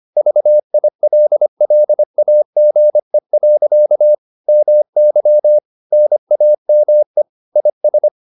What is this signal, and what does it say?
Morse code, heard as a keyed tone, about 25 wpm, 600 Hz. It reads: VILLAGE. MY NAME IS